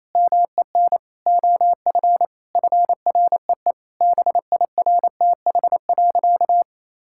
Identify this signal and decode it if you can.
Morse code, heard as a keyed tone, about 28 wpm, 700 Hz. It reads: MEN OF FREE BIRTH.
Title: Sentences from Top 1000 Words - Review Part 5 - 28wpm